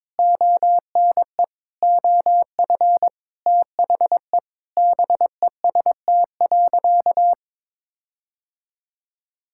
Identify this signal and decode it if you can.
Morse code, heard as a keyed tone, about 22 wpm, 700 Hz. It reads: ONE OF THE BEST.
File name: practice / 22wpm / Sentences with 4-Word Phrases - 22wpm